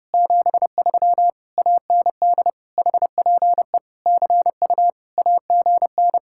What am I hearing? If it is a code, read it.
Morse code, heard as a keyed tone, about 31 wpm, 700 Hz. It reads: 73 AND HPE CU AGN